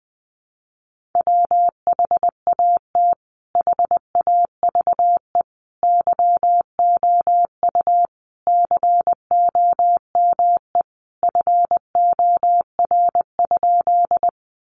Morse code, heard as a keyed tone, about 20 wpm, 700 Hz. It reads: WHAT HAVE YOU COME FOR?
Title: Sentences from Top 100 Words 20wpm